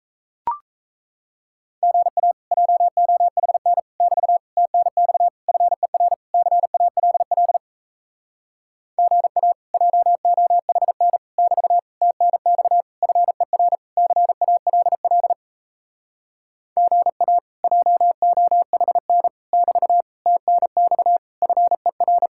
Morse code, heard as a keyed tone, about 37 wpm, 700 Hz. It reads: GA JOHN <BT> TNX FER CALL GA JOHN <BT> TNX FER CALL GA JOHN <BT> TNX FER